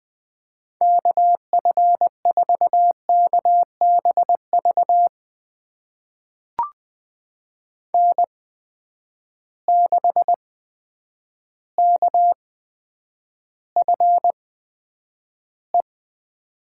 Morse code, heard as a keyed tone, about 20 wpm, spaced as for 9 wpm, 700 Hz. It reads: KF4KBV N6KFE
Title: USA Call Signs - 8x Character Spacing - 20wpm